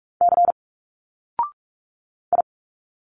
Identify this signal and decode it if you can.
Morse code, heard as a keyed tone, about 45 wpm, 700 Hz. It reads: C I
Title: Single Letters 45wpm